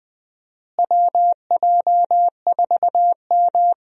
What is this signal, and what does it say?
Morse code, heard as a keyed tone, about 20 wpm, 700 Hz. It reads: WJ4M
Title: USA Call Signs - 4 Characters - Repeated 3x - 20wpm